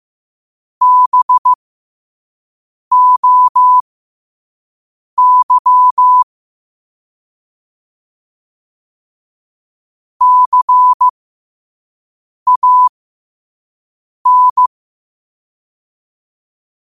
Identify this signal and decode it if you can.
Morse code, heard as a keyed tone, about 15 wpm, spaced as for 5 wpm, 1000 Hz. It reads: BOY CAN
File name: qotdfiles / 5wpm